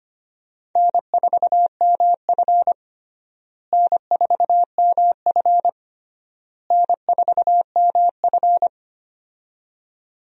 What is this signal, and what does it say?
Morse code, heard as a keyed tone, about 25 wpm, 700 Hz. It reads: N4MF N4MF N4MF
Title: USA Call Signs - 4 Characters - Repeated 3x - 25wpm